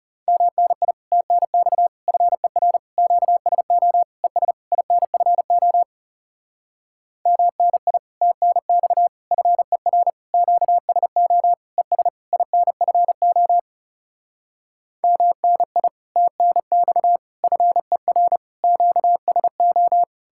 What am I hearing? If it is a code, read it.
Morse code, heard as a keyed tone, about 38 wpm, 700 Hz. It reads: MNI TNX FER QSO ES INFO MNI TNX FER QSO ES INFO MNI TNX FER QSO